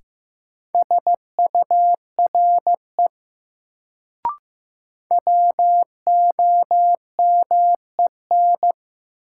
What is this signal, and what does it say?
Morse code, heard as a keyed tone, about 15 wpm, 700 Hz. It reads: SURE WOMEN
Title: Sets of 3 Words - New Word Drill - 15wpm